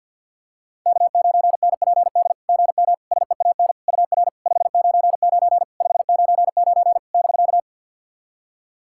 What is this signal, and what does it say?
Morse code, heard as a keyed tone, about 52 wpm, 690 Hz. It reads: K9NPD GM SEAN UR 599 599 <BK>
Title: SOTA QSO Practice 50wpm